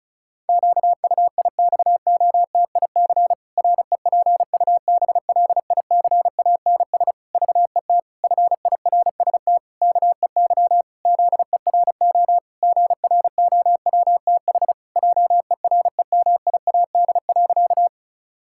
Morse code, heard as a keyed tone, about 36 wpm, 700 Hz. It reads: QUIXOTIC REPUBLICANS VET FIRST KEY ZERO GROWTH JEREMIAD.
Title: Pangram Challenge v2 - 35wpm